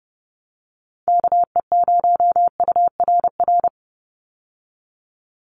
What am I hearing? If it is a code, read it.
Morse code, heard as a keyed tone, about 28 wpm, 700 Hz. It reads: KE0URR
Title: USA Call Signs 30wpm